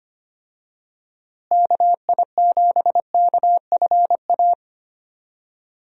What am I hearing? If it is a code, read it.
Morse code, heard as a keyed tone, about 25 wpm, 700 Hz. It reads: KI7KFA
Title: USA Call Signs - 6 Characters - Repeated 3x - 25wpm